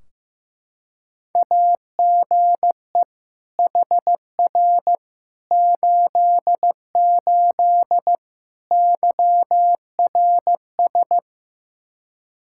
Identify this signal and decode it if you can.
Morse code, heard as a keyed tone, about 15 wpm, 700 Hz. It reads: AGE HR 88 YRS
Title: Practice QSO - Exchange 3 - More Info - 15wpm